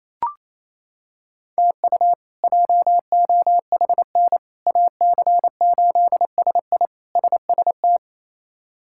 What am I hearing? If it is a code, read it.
Morse code, heard as a keyed tone, about 28 wpm, 700 Hz. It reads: TU JOHN AC8SI SST
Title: K1USN SST QSO Practice - 28wpm